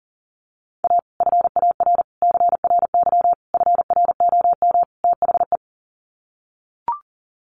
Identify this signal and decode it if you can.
Morse code, heard as a keyed tone, about 40 wpm, 700 Hz. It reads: A FAR CRY FROM THE
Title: Sets of 5 Words Encore 40wpm